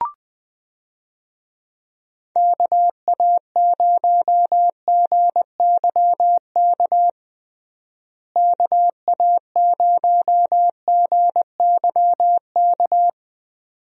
Morse code, heard as a keyed tone, about 20 wpm, 700 Hz. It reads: KA0GYK KA0GYK
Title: USA Call Signs - 6 Characters - Repeated 2x - 20wpm